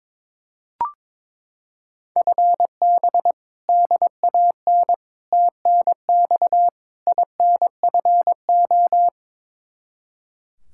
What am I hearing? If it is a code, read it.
Morse code, heard as a keyed tone, about 21 wpm, 700 Hz. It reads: FB DAN TNX INFO
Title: Practice QSO - Exchange 3 - More Info - 22wpm